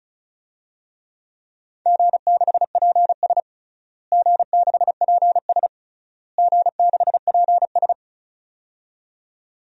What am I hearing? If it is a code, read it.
Morse code, heard as a keyed tone, about 36 wpm, 700 Hz. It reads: G6PS G6PS G6PS
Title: DXCC-100 Call Signs - 4 Characters - Repeated 3x - 35wpm